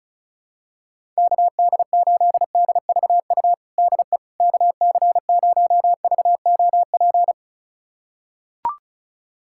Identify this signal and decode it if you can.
Morse code, heard as a keyed tone, about 36 wpm, 700 Hz. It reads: KD8DVU DE KC0VOP
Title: USA Callsign DE Callsign 35wpm